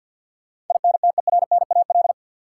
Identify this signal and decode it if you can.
Morse code, heard as a keyed tone, about 46 wpm, 700 Hz. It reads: INTERNAL
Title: Sentences from Top 1400 Words - New Word Drill Alt - 50wpm